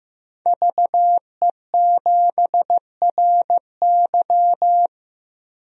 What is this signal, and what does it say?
Morse code, heard as a keyed tone, about 15 wpm, 700 Hz. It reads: VE7RY
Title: DXCC-100 Call Signs - 5 Characters - 15wpm